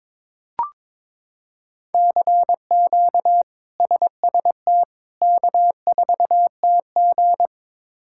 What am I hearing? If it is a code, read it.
Morse code, heard as a keyed tone, about 23 wpm, 700 Hz. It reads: CQ SST K4TG
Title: K1USN SST QSO Practice - 22wpm